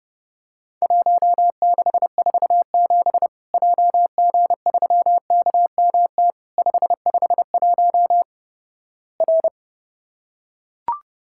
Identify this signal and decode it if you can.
Morse code, heard as a keyed tone, about 30 wpm, 700 Hz. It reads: 1647 JG3KMT 551 R
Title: Worked All Europe DX Contest - QTC Exchange - 30wpm